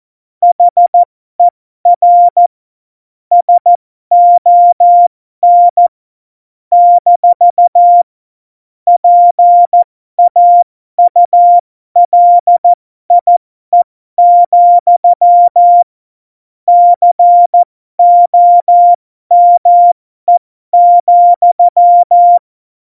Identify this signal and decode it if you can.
Morse code, heard as a keyed tone, about 14 wpm, spaced as for 12 wpm, 700 Hz. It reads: HER SON - PAULIE, COME,
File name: Morsekurs / Witze